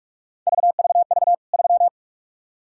Morse code, heard as a keyed tone, about 46 wpm, 700 Hz. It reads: VVV 3
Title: K1USN SST QSO Practice 45wpm